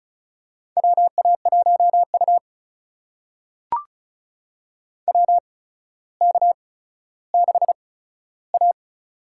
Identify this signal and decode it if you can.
Morse code, heard as a keyed tone, about 36 wpm, spaced as for 15 wpm, 700 Hz. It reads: WA1U WK6A
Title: USA Call Signs - 4 Characters - 8x Character Spacing - 35wpm